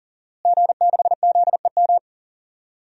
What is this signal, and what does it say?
Morse code, heard as a keyed tone, about 39 wpm, 700 Hz. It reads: G6ZEM
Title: DXCC-100 Call Signs - 5 Characters - Repeated 3x - 40wpm